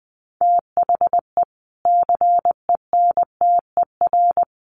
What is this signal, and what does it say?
Morse code, heard as a keyed tone, about 20 wpm, 700 Hz. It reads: THE CENTER
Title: Sets of 3 Words 20wpm